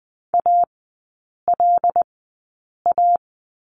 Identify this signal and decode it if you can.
Morse code, heard as a keyed tone, about 20 wpm, spaced as for 8 wpm, 700 Hz. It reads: ALA
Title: 074 Lesson 13e - Letter L - Three Characters 20wpm